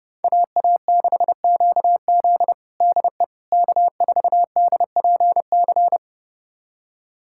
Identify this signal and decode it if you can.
Morse code, heard as a keyed tone, about 30 wpm, 700 Hz. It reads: AA6QZ DE K4DPC
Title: USA Callsign DE Callsign - 5 Characters - 30wpm